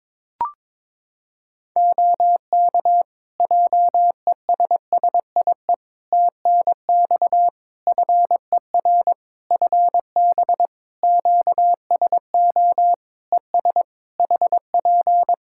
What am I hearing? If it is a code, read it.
Morse code, heard as a keyed tone, about 22 wpm, 700 Hz. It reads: OK JESSIE TNX FER FB QSO ES HP
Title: Practice QSO - Exchange 4 - Ending - 22wpm